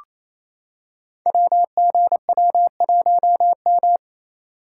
Morse code, heard as a keyed tone, about 28 wpm, 700 Hz. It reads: WGW1M
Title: Code Groups - 5 Letter:Numbers - 28wpm